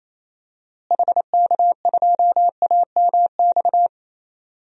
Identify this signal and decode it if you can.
Morse code, heard as a keyed tone, about 27 wpm, 700 Hz. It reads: HK2AMX
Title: DXCC-100 Call Signs - 6 Characters - Repeated 2x - 28wpm